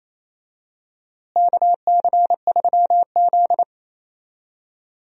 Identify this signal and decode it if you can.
Morse code, heard as a keyed tone, about 27 wpm, 700 Hz. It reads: KC3Z
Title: USA Call Signs - 4 Characters - Repeated 3x - 28wpm